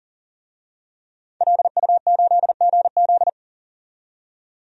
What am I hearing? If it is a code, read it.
Morse code, heard as a keyed tone, about 40 wpm, 700 Hz. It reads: LU8GZ
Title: DXCC-100 Call Signs - 5 Characters - 40wpm